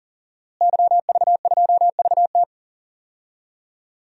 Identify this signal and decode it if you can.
Morse code, heard as a keyed tone, about 40 wpm, 700 Hz. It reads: YV2VT
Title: DXCC-100 Call Signs - 5 Characters - Repeated 3x - 40wpm